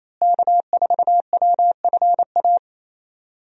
Morse code, heard as a keyed tone, about 28 wpm, 700 Hz. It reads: K4WFA
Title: USA Call Signs - 5 Characters - Repeated 2x - 28wpm